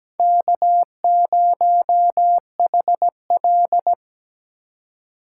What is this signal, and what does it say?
Morse code, heard as a keyed tone, about 17 wpm, 700 Hz. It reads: K0HL
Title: USA Call Signs - 4 Characters - 6x Character Spacing - 17wpm